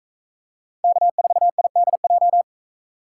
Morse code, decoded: K4IDJ

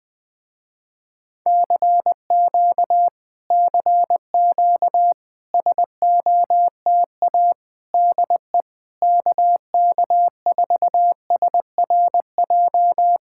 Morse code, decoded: CQ CQ SOTA DE KK4SRJ

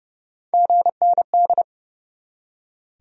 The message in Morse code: GND

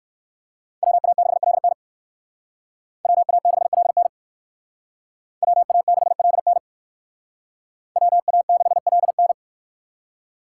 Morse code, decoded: WA6LN WA6LN WA6LN WA6LN

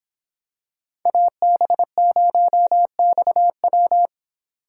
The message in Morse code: AB0XW

